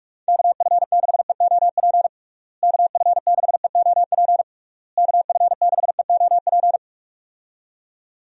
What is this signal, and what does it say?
Morse code, heard as a keyed tone, about 41 wpm, 700 Hz. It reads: KF6EOP KF6EOP KF6EOP
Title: USA Call Signs - 6 Characters - Repeated 3x - 45wpm